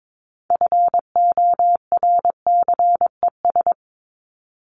Morse code, heard as a keyed tone, about 22 wpm, 700 Hz. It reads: FORCES
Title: ES:S Suffix - Top 100 Words with Base Words 22wpm